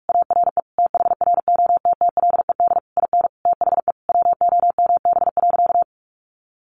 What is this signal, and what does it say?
Morse code, heard as a keyed tone, about 44 wpm, 700 Hz. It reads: ARE THROTTLED IN THE WOMB.
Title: A Sound Of Thunder 45wpm